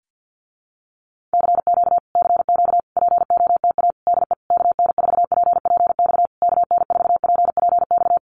Morse code, heard as a keyed tone, about 50 wpm, 700 Hz. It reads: CQ CQ POTA DE KN4PPX KN4PPX
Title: Common QSO Phrases Encore 50wpm